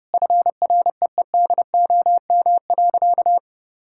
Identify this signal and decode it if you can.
Morse code, heard as a keyed tone, about 32 wpm, 700 Hz. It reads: FREEDOM.
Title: Sentences from Top 1000 Words - Review Part 5 - 30wpm